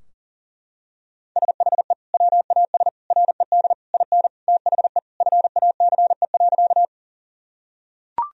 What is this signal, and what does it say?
Morse code, heard as a keyed tone, about 40 wpm, 700 Hz. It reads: SHE WAS RED IN THE FACE.
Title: Sentences from Top 1000 Words - Review Part 2 - 40wpm